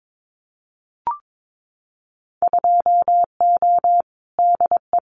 Morse code, heard as a keyed tone, about 22 wpm, 700 Hz. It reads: 2O DE
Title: Winter Field Day Exchanges Encore 22wpm